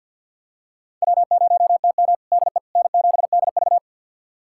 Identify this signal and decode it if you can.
Morse code, heard as a keyed tone, about 46 wpm, 700 Hz. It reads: W0TM DE N7DV
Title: USA Callsign DE Callsign - 4 Characters - 50wpm